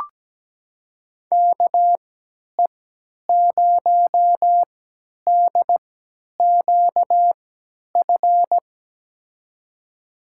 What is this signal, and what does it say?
Morse code, heard as a keyed tone, about 17 wpm, spaced as for 10 wpm, 700 Hz. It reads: KE0DQF
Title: USA Call Signs - 6 Characters - 3x Character Spacing - 17wpm